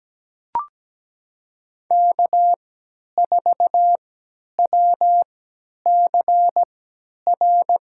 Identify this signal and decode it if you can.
Morse code, heard as a keyed tone, about 17 wpm, spaced as for 10 wpm, 700 Hz. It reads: K4WCR
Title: USA Call Signs - 3x Character Spacing - 17wpm